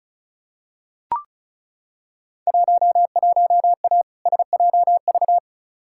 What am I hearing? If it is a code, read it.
Morse code, heard as a keyed tone, about 36 wpm, 700 Hz. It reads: 11A SJV
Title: Field Day Exchanges Encore - 35wpm